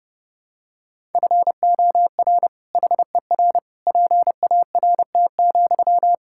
Morse code, heard as a keyed tone, about 30 wpm, 700 Hz. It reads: FOR HER PART,
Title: Sentences from Top 1000 Words - Review Part 5 - 30wpm